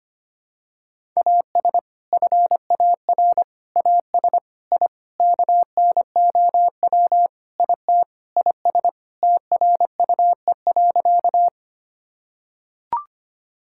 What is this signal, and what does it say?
Morse code, heard as a keyed tone, about 25 wpm, 700 Hz. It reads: AS FAR AS I KNOW IT IS TRUE.
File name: practice / 25wpm / Sentences with 4-Word Phrases - 25wpm